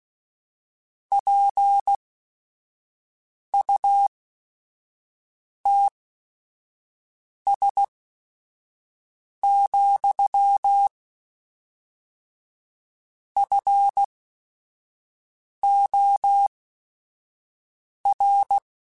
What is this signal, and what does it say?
Morse code, heard as a keyed tone, about 16 wpm, spaced as for 5 wpm, 800 Hz. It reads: PUTS, FOR
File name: Technical Subjects / Technical Subjects-mp3s / AND Gate @5